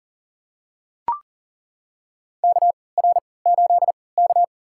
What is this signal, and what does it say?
Morse code, heard as a keyed tone, about 40 wpm, spaced as for 23 wpm, 700 Hz. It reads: KR8K
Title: USA Call Signs - 4 Characters - 3x Character Spacing - 40wpm